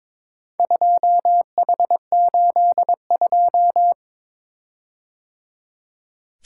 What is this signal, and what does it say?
Morse code, heard as a keyed tone, about 22 wpm, 700 Hz. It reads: 2H82
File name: practice / 22wpm / Four Letter-Number Combinations - 22wpm